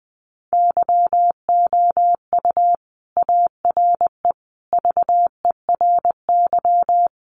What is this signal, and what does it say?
Morse code, heard as a keyed tone, about 20 wpm, 700 Hz. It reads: YOU ARE VERY
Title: Sentences from Top 1400 Words 20wpm